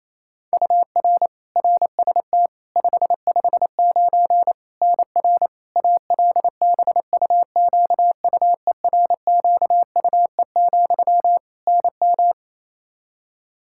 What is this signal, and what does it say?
Morse code, heard as a keyed tone, about 27 wpm, 700 Hz. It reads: UR RST 559 NR ALBUQUERQUE, NM
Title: Practice QSO - Exchange 1 - Beginning - 28wpm